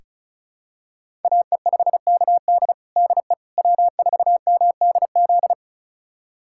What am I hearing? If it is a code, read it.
Morse code, heard as a keyed tone, about 36 wpm, 700 Hz. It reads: AE5KD DE W4MDZ
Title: USA Callsign DE Callsign - 5 Characters - 35wpm